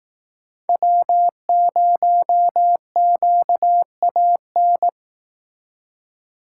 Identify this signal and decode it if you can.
Morse code, heard as a keyed tone, about 18 wpm, 700 Hz. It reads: W0QAN
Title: Speed Racing - USA 1x3 Call Signs 15wpm